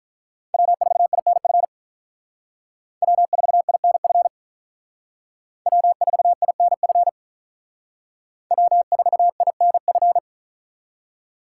Morse code, heard as a keyed tone, about 46 wpm, 700 Hz. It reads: W4INF W4INF W4INF W4INF